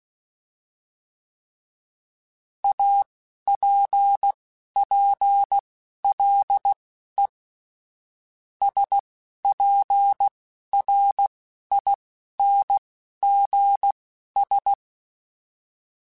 Morse code, heard as a keyed tone, about 16 wpm, spaced as for 11 wpm, 800 Hz. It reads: APPLE SPRINGS
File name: fromSMs / FromSMs mp3s / Cities@10